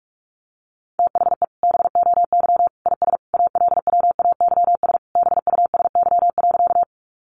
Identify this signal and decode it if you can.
Morse code, heard as a keyed tone, about 44 wpm, 700 Hz. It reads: THE BOY IS ALWAYS BUSY.
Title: Additional Easy Phrases 45wpm